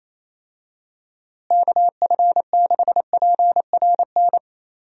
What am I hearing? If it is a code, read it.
Morse code, heard as a keyed tone, about 28 wpm, 700 Hz. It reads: KF6PRN
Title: USA Call Signs - 6 Characters - Repeated 2x - 28wpm